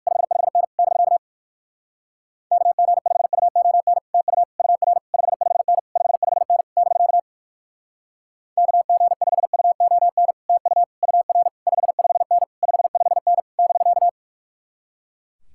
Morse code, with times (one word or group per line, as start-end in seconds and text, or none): 0.1-0.6 55N
0.8-1.2 <BK>
2.5-4.0 KG5UON
4.1-4.4 TU
4.6-5.0 UR
5.1-5.8 55N
6.0-6.6 55N
6.8-7.2 <BK>
8.6-10.3 KG5UON
10.5-10.8 TU
11.0-11.5 UR
11.7-12.4 55N
12.6-13.4 55N
13.6-14.1 <BK>